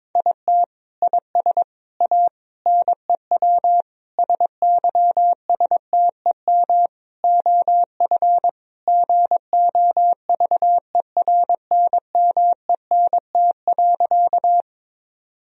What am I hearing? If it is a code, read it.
Morse code, heard as a keyed tone, about 21 wpm, 700 Hz. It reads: IT IS A NEW SYSTEM OF GOVERNMENT.
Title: Sentences from Top 1000 Words - Review Part 3 - 22wpm